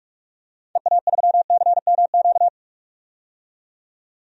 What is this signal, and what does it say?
Morse code, heard as a keyed tone, about 42 wpm, 700 Hz. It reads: EA3CMQ
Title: DXCC-100 Call Signs - 6 Characters - 45wpm